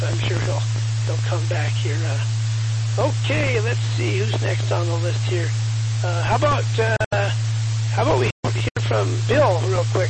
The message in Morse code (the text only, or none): none